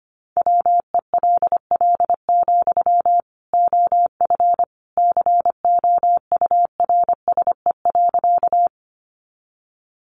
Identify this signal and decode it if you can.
Morse code, heard as a keyed tone, about 25 wpm, 700 Hz. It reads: WELL, OF COURSE.
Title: Sentences from Top 500 Words 25wpm